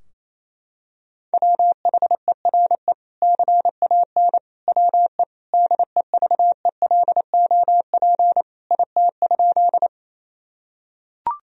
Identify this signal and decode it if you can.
Morse code, heard as a keyed tone, about 28 wpm, 700 Hz. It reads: WHERE CAN WE DEVELOP IT?
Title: Sentences from Top 1000 Words - Review Part 2 - 28wpm